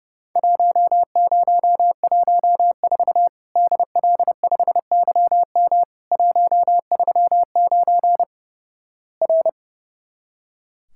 Morse code, heard as a keyed tone, about 31 wpm, 700 Hz. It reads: 1014 DL5YM 139 R
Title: Worked All Europe DX Contest - QTC Exchange - 30wpm